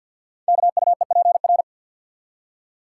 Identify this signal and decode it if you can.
Morse code, heard as a keyed tone, about 48 wpm, 690 Hz. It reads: KUEPR
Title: Code Groups - 5 Letters - 50wpm